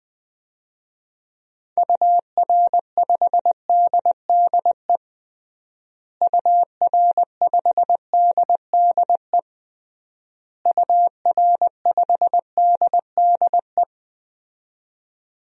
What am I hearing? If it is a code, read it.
Morse code, heard as a keyed tone, about 21 wpm, 700 Hz. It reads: UR5DDE UR5DDE UR5DDE
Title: DXCC-100 Call Signs - 6 Characters - Repeated 3x - 20wpm